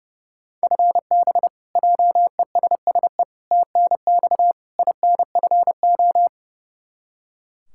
Morse code, heard as a keyed tone, about 30 wpm, 700 Hz. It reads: FB JESSE TNX INFO
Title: K1USN SST QSO Practice 30wpm